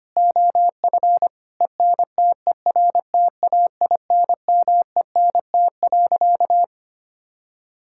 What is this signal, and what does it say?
Morse code, heard as a keyed tone, about 25 wpm, 700 Hz. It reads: OF ENTERTAINMENT.
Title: Sentences from Top 1000 Words - Review Part 6 - 25wpm